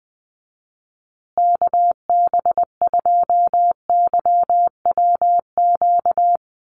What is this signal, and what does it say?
Morse code, heard as a keyed tone, about 20 wpm, 700 Hz. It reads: KB2YWQ
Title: USA Call Signs Repeated 2x 20wpm